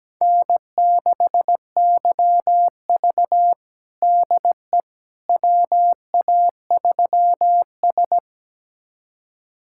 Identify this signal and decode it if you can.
Morse code, heard as a keyed tone, about 17 wpm, 700 Hz. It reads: N6YV DE WA3S
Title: USA Callsign DE Callsign - 4 Characters - 17wpm